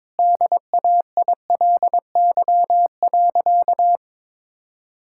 Morse code, heard as a keyed tone, about 22 wpm, 700 Hz. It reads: DAILY.
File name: practice / 22wpm / Easy Sentences with Hints - 22wpm